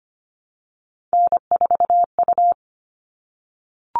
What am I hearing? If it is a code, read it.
Morse code, heard as a keyed tone, about 25 wpm, 700 Hz. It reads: N4U
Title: All Cut Numbers - 3-Digit 25wpm